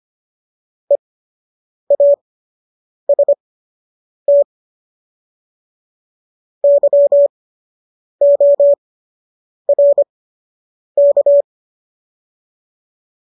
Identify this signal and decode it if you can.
Morse code, heard as a keyed tone, about 25 wpm, spaced as for 8 wpm, 570 Hz. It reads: EAST YORK